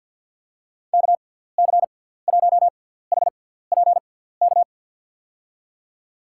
Morse code, heard as a keyed tone, about 51 wpm, spaced as for 17 wpm, 700 Hz. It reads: KC1HPK
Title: USA Call Signs - 6 Characters - 6x Character Spacing - 50wpm